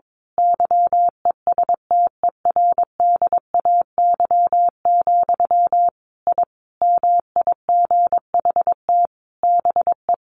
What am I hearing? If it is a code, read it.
Morse code, heard as a keyed tone, about 22 wpm, 700 Hz. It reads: YESTERDAY, I MIGHT BE
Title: A Sound Of Thunder 22wpm